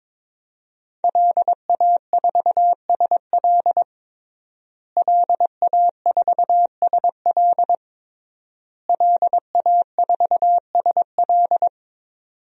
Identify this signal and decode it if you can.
Morse code, heard as a keyed tone, about 22 wpm, 700 Hz. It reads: LA4SL LA4SL LA4SL
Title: DXCC-100 Call Signs - 5 Characters - Repeated 3x - 22wpm